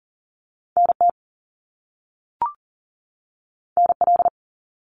Morse code, heard as a keyed tone, about 42 wpm, 700 Hz. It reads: NT NL
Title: Canadian Provinces and Territories 40wpm